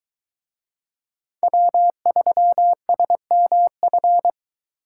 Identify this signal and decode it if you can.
Morse code, heard as a keyed tone, about 22 wpm, 700 Hz. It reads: W3SMF